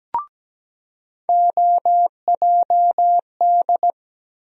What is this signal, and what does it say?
Morse code, heard as a keyed tone, about 17 wpm, 700 Hz. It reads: OJD
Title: USA Call Sign Postfixes - 3 Letters 17wpm